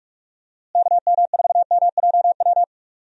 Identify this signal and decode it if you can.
Morse code, heard as a keyed tone, about 46 wpm, 700 Hz. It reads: KM4MJW
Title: USA Call Signs - 6 Characters - Repeated 3x - 45wpm